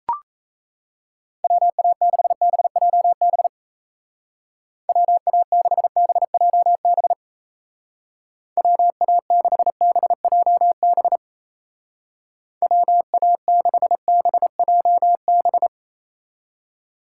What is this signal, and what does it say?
Morse code, heard as a keyed tone, about 34 wpm, 700 Hz. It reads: WA6BJB WA6BJB WA6BJB WA6BJB
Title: Speed Racing - USA Call Signs 28wpm